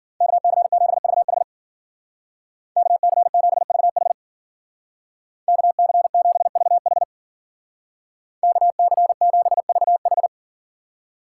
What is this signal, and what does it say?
Morse code, heard as a keyed tone, about 50 wpm, 700 Hz. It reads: KC7VH KC7VH KC7VH KC7VH